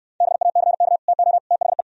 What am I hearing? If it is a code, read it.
Morse code, heard as a keyed tone, about 67 wpm, 700 Hz. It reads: BACK TO THE